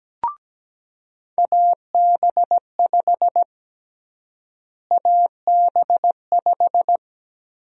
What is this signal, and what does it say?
Morse code, heard as a keyed tone, about 17 wpm, 700 Hz. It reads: AB5 AB5
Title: USA Call Sign Prefixes - Repeated 2x - 17wpm